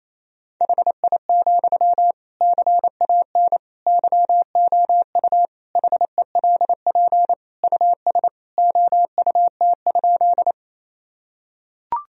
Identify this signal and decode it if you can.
Morse code, heard as a keyed tone, about 28 wpm, 700 Hz. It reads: HI, CAN YOU HELP US OUT?
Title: Sentences from Top 1000 Words - Review Part 3 - 28wpm